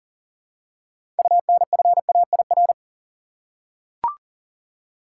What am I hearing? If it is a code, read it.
Morse code, heard as a keyed tone, about 40 wpm, 700 Hz. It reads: UNFAIR
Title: Six Letter Words - 40wpm